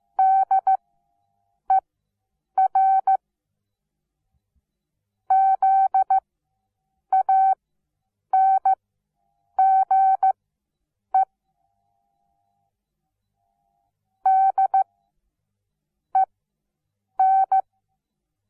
Morse code, heard as a keyed tone, about 15 wpm, spaced as for 7 wpm, 770 Hz. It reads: DER ZANGE DEN
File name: Onlinelehrgang / morsen / DJ4UF Morsen17